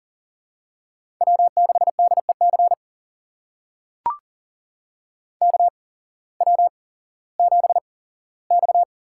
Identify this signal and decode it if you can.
Morse code, heard as a keyed tone, about 39 wpm, spaced as for 19 wpm, 700 Hz. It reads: W6DEC KW7X
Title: USA Call Signs - 5 Characters - 8x Character Spacing - 40wpm